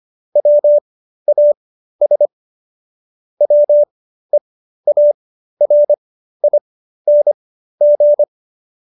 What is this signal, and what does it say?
Morse code, heard as a keyed tone, about 25 wpm, spaced as for 13 wpm, 590 Hz. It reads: WAS WEARING